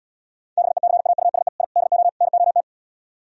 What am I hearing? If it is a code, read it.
Morse code, heard as a keyed tone, about 77 wpm, 700 Hz. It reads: BECAUSE I DO NOT